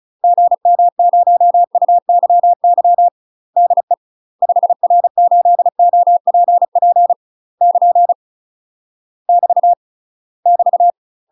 Morse code, heard as a keyed tone, about 34 wpm, 700 Hz. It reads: GM0UYY DE 5R8OPP <KN> <BT> <BT>
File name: UK MASTER QSOs TRAINING / 35WPM / UK MASTER QSOs 0043